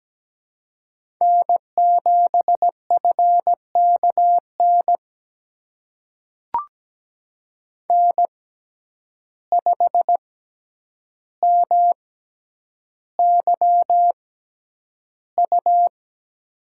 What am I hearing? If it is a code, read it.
Morse code, heard as a keyed tone, about 17 wpm, spaced as for 9 wpm, 700 Hz. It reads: N7FKN N5MYU
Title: USA Call Signs - 5 Characters - 6x Character Spacing - 17wpm